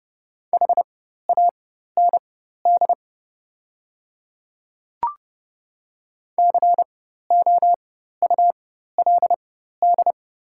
Morse code, heard as a keyed tone, about 30 wpm, spaced as for 14 wpm, 700 Hz. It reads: HAND COULD
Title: 4x Character Spacing - Top 200 Words - 30wpm